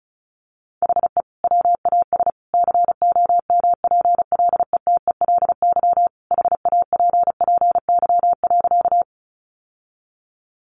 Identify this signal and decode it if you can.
Morse code, heard as a keyed tone, about 34 wpm, 700 Hz. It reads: HE WAS COMPLETELY HAPPY.
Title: Sentences from Top 1100 Words 35wpm